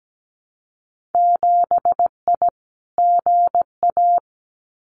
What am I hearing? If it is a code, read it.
Morse code, heard as a keyed tone, about 17 wpm, 700 Hz. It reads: 7I GA
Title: Winter Field Day Exchanges Encore 17wpm